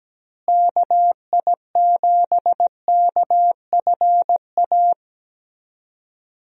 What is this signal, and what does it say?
Morse code, heard as a keyed tone, about 17 wpm, 700 Hz. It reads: KI7KFA
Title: USA Call Signs - 6 Characters - Repeated 3x - 17wpm